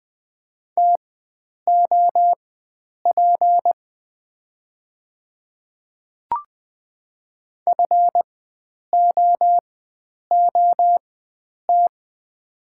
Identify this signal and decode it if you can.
Morse code, heard as a keyed tone, about 20 wpm, spaced as for 9 wpm, 700 Hz. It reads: TOP FOOT
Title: 4x Character Spacing - Top 500 Words - 20wpm